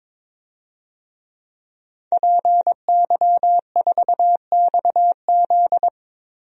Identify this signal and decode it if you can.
Morse code, heard as a keyed tone, about 21 wpm, 700 Hz. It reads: PY4XZ
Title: DXCC-100 Call Signs - 5 Characters - 22wpm